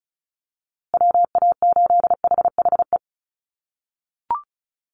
Morse code, heard as a keyed tone, about 34 wpm, 700 Hz. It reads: WA8HHE
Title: USA Call Signs Repeated 2x 35wpm